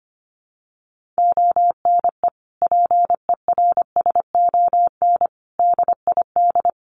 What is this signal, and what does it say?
Morse code, heard as a keyed tone, about 25 wpm, 700 Hz. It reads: ONE PERSON DID
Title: Sentences from Top 1500 Words 25wpm